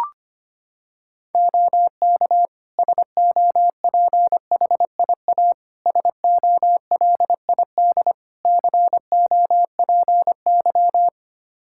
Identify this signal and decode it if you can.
Morse code, heard as a keyed tone, about 25 wpm, 700 Hz. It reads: OK SOPHIA SOLID COPY
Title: Practice QSO - Exchange 3 - More Info - 25wpm